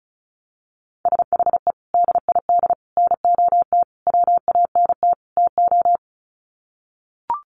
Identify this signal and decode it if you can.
Morse code, heard as a keyed tone, about 34 wpm, 700 Hz. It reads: SHE DID NOT WANT TO
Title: Sets of 5 Words 35wpm